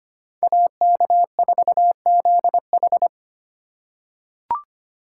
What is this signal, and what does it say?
Morse code, heard as a keyed tone, about 25 wpm, 700 Hz. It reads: AK4ZH
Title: USA Call Signs - 5 Characters - 2x Character Spacing - 25wpm